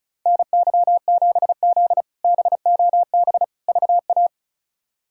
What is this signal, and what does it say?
Morse code, heard as a keyed tone, about 34 wpm, 700 Hz. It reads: NY7Z BOB VA
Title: K1USN SST QSO Practice - 35wpm